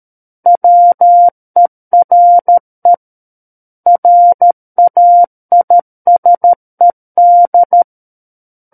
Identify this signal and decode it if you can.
Morse code, heard as a keyed tone, about 13 wpm, 700 Hz. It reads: WERE RAISED
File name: Book One / 04-Chapter 4 - 13 WPM